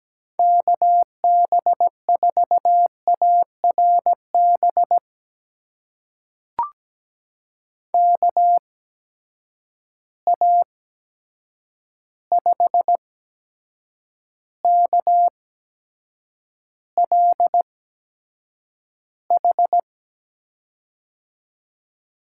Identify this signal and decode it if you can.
Morse code, heard as a keyed tone, about 17 wpm, spaced as for 7 wpm, 700 Hz. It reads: KB4ARB KA5KLH
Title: USA Call Signs - 6 Characters - 8x Character Spacing - 17wpm